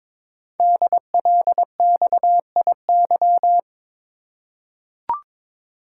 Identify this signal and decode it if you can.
Morse code, heard as a keyed tone, about 22 wpm, 700 Hz. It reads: DLXIY
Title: Code Groups - 5 Letters - 22wpm